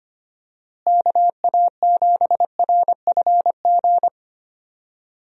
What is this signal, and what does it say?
Morse code, heard as a keyed tone, about 25 wpm, 700 Hz. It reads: KA7RFG
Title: USA Call Signs - 6 Characters - Repeated 2x - 25wpm